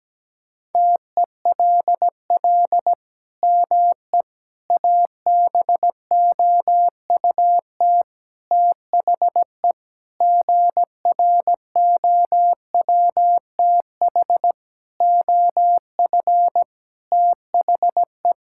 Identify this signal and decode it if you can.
Morse code, heard as a keyed tone, about 17 wpm, 700 Hz. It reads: TELL ME ABOUT THE GROWTH OF THE
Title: Sentences from Top 1000 Words - Review Part 5 - 17wpm